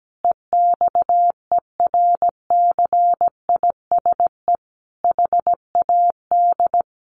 Morse code, heard as a keyed tone, about 17 wpm, 700 Hz. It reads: EXERCISE HAD
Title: Sentences from Top 1400 Words 17wpm